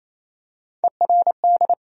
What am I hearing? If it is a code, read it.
Morse code, heard as a keyed tone, about 28 wpm, 700 Hz. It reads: ERD
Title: USA Call Sign Postfixes - 3 Letters 28wpm